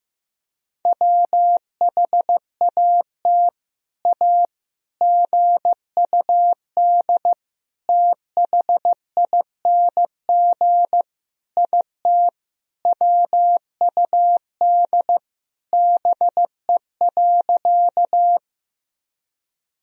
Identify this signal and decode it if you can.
Morse code, heard as a keyed tone, about 15 wpm, 700 Hz. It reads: WHAT A GUD THING IT WUD BE.
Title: Sentences from Top 200 Words and Abbreviations 15wpm